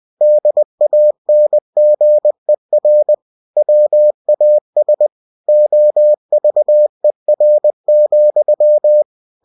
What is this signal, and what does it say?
Morse code, heard as a keyed tone, about 20 wpm, 600 Hz. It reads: DANGER WAS OVER,